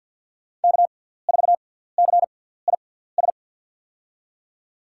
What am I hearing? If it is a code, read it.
Morse code, heard as a keyed tone, about 49 wpm, spaced as for 17 wpm, 700 Hz. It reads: K4CIS
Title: USA Call Signs - 6x Character Spacing - 50wpm